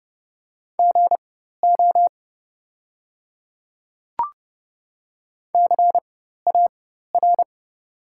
Morse code, heard as a keyed tone, about 30 wpm, spaced as for 14 wpm, 700 Hz. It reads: GO CAR
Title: 4x Character Spacing - Top 300 Words - 30wpm